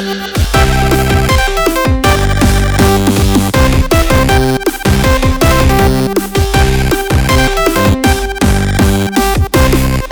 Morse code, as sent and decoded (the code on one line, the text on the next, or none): none